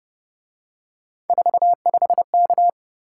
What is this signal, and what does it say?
Morse code, heard as a keyed tone, about 26 wpm, 700 Hz. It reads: 45K